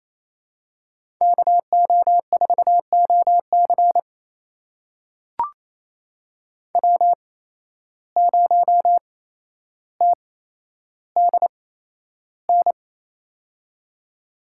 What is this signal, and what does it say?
Morse code, heard as a keyed tone, about 28 wpm, spaced as for 12 wpm, 700 Hz. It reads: KO4OC W0TDN
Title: USA Call Signs - 5 Characters - 8x Character Spacing - 28wpm